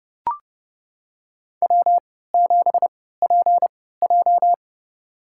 Morse code, heard as a keyed tone, about 32 wpm, spaced as for 17 wpm, 700 Hz. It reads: W7PJ
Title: USA Call Signs - 4 Characters - 3x Character Spacing - 30wpm